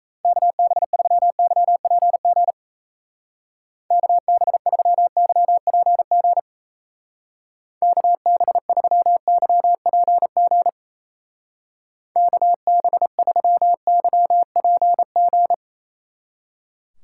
Morse code, heard as a keyed tone, about 34 wpm, 700 Hz. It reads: KB3YPG KB3YPG KB3YPG KB3YPG
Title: Speed Racing - USA Call Signs 28wpm